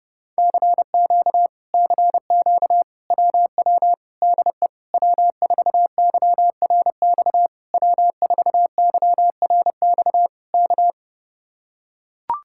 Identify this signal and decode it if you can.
Morse code, heard as a keyed tone, about 30 wpm, 700 Hz. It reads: CQ CQ WW DE W4YRX W4YRX K